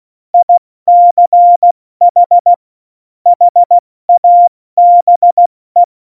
Morse code, heard as a keyed tone, about 16 wpm, 700 Hz. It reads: ICH HABE
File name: Morsekurs / Witze / Witz 22. September 2020, 70 F80